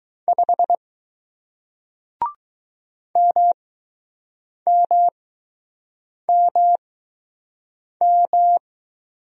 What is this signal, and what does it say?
Morse code, heard as a keyed tone, about 20 wpm, 700 Hz. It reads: 5 M M M M